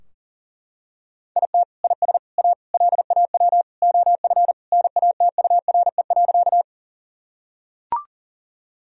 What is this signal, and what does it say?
Morse code, heard as a keyed tone, about 40 wpm, 700 Hz. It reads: IT IS A LAW OF NATURE.
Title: Sentences from Top 1000 Words - Review Part 3 - 40wpm